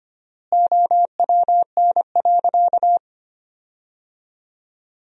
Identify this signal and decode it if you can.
Morse code, heard as a keyed tone, about 25 wpm, 700 Hz. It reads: OWN.
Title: 4x Word-Spacing - Sentences from Top 300 Words - 25wpm